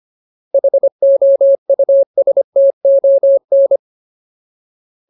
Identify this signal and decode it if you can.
Morse code, heard as a keyed tone, about 25 wpm, 550 Hz. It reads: HOUSTON
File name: Ultimate Word List / 25WPM Ultimate